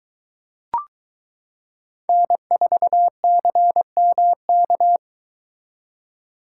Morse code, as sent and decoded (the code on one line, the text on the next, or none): -. ....- -.-. -- -.-
N4CMK